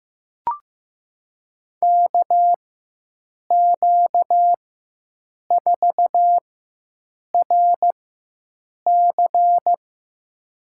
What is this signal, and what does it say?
Morse code, heard as a keyed tone, about 15 wpm, spaced as for 7 wpm, 700 Hz. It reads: KQ4RC
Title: USA Call Signs - 5 Characters - 4x Character Spacing - 15wpm